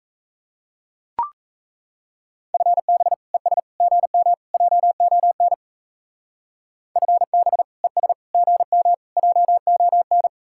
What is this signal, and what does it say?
Morse code, heard as a keyed tone, about 34 wpm, 700 Hz. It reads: FB ES GM JON FB ES GM JON
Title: Speed-Racing Common QSO Phrases - 28wpm